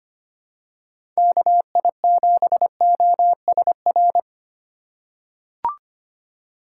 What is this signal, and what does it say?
Morse code, heard as a keyed tone, about 25 wpm, 700 Hz. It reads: KI7OSR
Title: USA Call Signs - 6 Characters - 4x Character Spacing - 25wpm